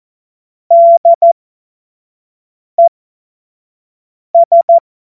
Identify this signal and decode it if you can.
Morse code, heard as a keyed tone, about 14 wpm, spaced as for 5 wpm, 680 Hz. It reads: DES
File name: Morsekurs / Witze / Witz 11. Juli 2023 25 F70